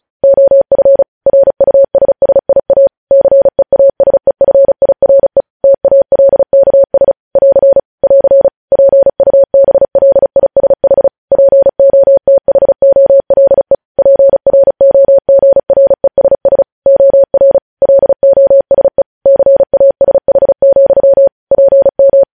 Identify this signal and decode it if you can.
Morse code, heard as a keyed tone, about 36 wpm, 550 Hz. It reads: OF RUSSIA CEASEFIRE TALKS <AR> <AR> PUBLISH POTHOLE PROGRESS OR LOSE CASH, PM
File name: podcast / News headlines-1x-35x35